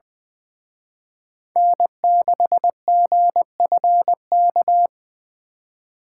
Morse code, heard as a keyed tone, about 23 wpm, 700 Hz. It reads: N6GFK
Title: Speed Racing - USA 1x3 Call Signs 20wpm